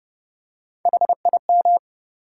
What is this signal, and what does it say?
Morse code, heard as a keyed tone, about 30 wpm, 700 Hz. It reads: HIM
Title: Three Letter Words - 30wpm